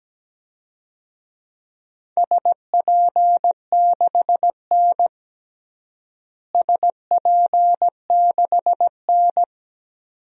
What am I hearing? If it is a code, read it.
Morse code, heard as a keyed tone, about 17 wpm, 700 Hz. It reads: SP6N SP6N
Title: DXCC-100 Call Signs - 4 Characters - Repeated 2x - 17wpm